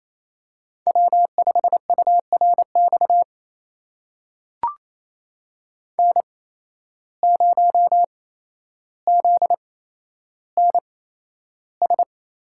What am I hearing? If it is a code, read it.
Morse code, heard as a keyed tone, about 28 wpm, spaced as for 12 wpm, 700 Hz. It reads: W5URX N0ZNS
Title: USA Call Signs - 5 Characters - 8x Character Spacing - 28wpm